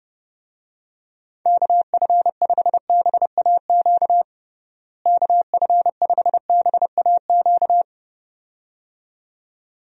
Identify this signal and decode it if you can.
Morse code, heard as a keyed tone, about 30 wpm, 700 Hz. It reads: KF5BAQ KF5BAQ
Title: USA Call Signs - 6 Characters - Repeated 2x - 30wpm